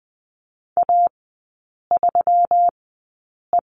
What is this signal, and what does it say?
Morse code, heard as a keyed tone, about 20 wpm, spaced as for 8 wpm, 690 Hz. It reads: A3E
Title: 145 Lesson 24e - Letter y - Three Characters 20wpm